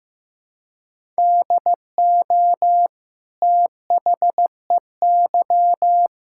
Morse code, heard as a keyed tone, about 15 wpm, 700 Hz. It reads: DO THEY
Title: Sentences from Top 1000 Words - Review Part 6 - 15wpm